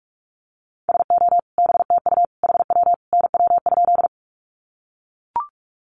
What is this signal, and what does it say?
Morse code, heard as a keyed tone, about 44 wpm, 700 Hz. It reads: SO BTU HW NW?
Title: Common QSO Phrases 45wpm